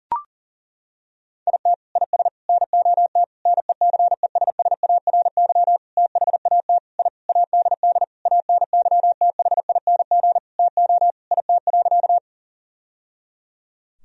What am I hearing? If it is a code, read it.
Morse code, heard as a keyed tone, about 40 wpm, 700 Hz. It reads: IT IS NOT NECESSARY THAT I ADD ANYTHING TO IT.
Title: Sentences from Top 1000 Words - Review Part 1 - 40wpm